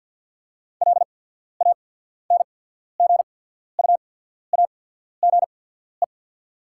Morse code, heard as a keyed tone, about 48 wpm, spaced as for 14 wpm, 690 Hz. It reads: LANGUAGE
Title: 8x Character Spacing - Top 500 Words - 50wpm